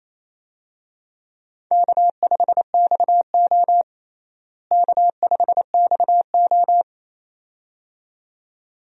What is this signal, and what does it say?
Morse code, heard as a keyed tone, about 28 wpm, 700 Hz. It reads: K5XO K5XO
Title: USA Call Signs - 4 Characters - Repeated 2x - 28wpm